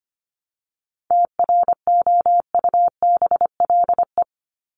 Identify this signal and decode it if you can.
Morse code, heard as a keyed tone, about 25 wpm, 700 Hz. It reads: TROUBLE
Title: Top 1001-1250 Words 25wpm